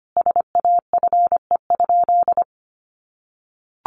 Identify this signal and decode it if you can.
Morse code, heard as a keyed tone, about 25 wpm, 700 Hz. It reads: SAFE?
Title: Sentences from Top 900 Words 25wpm